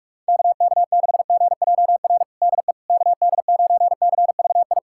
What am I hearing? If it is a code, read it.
Morse code, heard as a keyed tone, about 46 wpm, 700 Hz. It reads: KK6GJR DE KD9CVI
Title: USA Callsign DE Callsign - 6 Characters - 45wpm